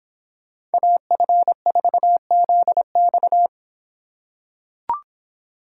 Morse code, heard as a keyed tone, about 24 wpm, 700 Hz. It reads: AF4ZX